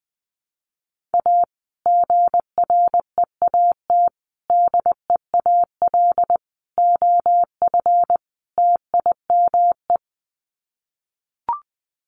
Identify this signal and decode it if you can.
Morse code, heard as a keyed tone, about 20 wpm, 700 Hz. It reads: A GREAT DEAL OF TIME
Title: Sets of 5 Words Encore2 20wpm